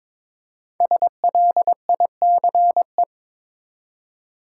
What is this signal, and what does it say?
Morse code, heard as a keyed tone, about 22 wpm, 700 Hz. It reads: SLICE
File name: 5 Letter Words - First 3 Letters Given - 22wpm